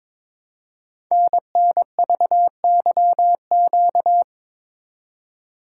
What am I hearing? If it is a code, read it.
Morse code, heard as a keyed tone, about 22 wpm, 690 Hz. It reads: NNVYQ
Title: Code Groups - 5 Letters - 22wpm